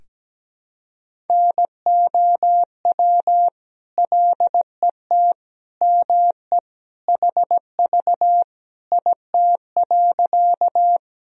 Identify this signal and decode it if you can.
Morse code, heard as a keyed tone, about 17 wpm, 700 Hz. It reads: NOW LET ME HV IT.
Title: Sentences from Top 300 Words and Abbreviations 17wpm